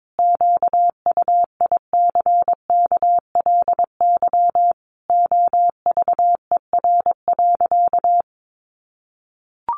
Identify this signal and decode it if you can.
Morse code, heard as a keyed tone, about 22 wpm, 700 Hz. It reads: QUICKLY OVER.
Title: Sentences from Top 1100 Words 22wpm